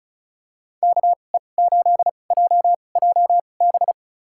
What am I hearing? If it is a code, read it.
Morse code, heard as a keyed tone, about 36 wpm, spaced as for 25 wpm, 700 Hz. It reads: KE8JJB
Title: USA Call Signs - 6 Characters - 2x Character Spacing - 35wpm